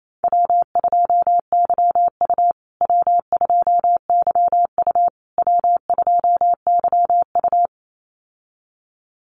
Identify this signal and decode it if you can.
Morse code, heard as a keyed tone, about 28 wpm, 700 Hz. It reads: W2YU W2YU W2YU
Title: USA Call Signs Repeated 3x 28wpm